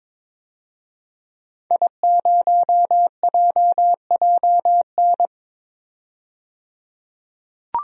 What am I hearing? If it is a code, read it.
Morse code, heard as a keyed tone, about 22 wpm, 700 Hz. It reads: I0JJN
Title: DXCC-100 Call Signs - 5 Characters - 22wpm